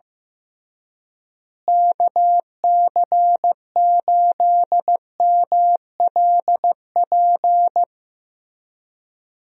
Morse code, heard as a keyed tone, about 15 wpm, 700 Hz. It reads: KC8MLP
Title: USA Call Signs - 6 Characters - Repeated 3x - 15wpm